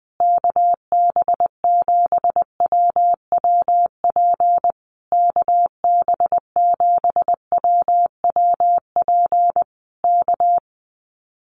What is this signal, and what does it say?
Morse code, heard as a keyed tone, about 19 wpm, 700 Hz. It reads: KB7WWP KB7WWP K
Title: Common QSO Phrases Encore 20wpm